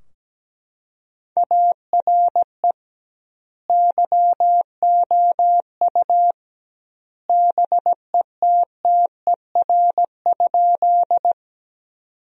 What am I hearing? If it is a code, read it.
Morse code, heard as a keyed tone, about 17 wpm, 700 Hz. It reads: ARE YOU BETTER?
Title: 2x Word-Spacing - Sentences from Top 400 Words - 17wpm